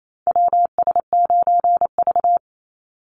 Morse code, heard as a keyed tone, about 28 wpm, 700 Hz. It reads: WS9V
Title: Call Signs Repeated 2x - 28wpm